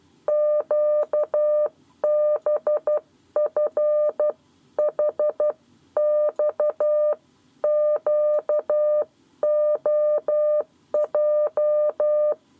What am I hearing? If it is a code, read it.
Morse code, heard as a keyed tone, about 12 wpm, spaced as for 10 wpm, 600 Hz. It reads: QBFHXQOJ
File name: 7 Lezione cw 1